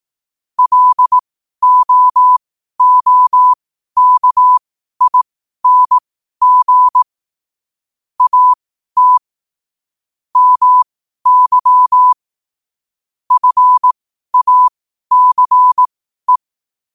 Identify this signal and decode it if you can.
Morse code, heard as a keyed tone, about 18 wpm, spaced as for 12 wpm, 1000 Hz. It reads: LOOKING AT MY FACE
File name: qotdfiles / 12wpm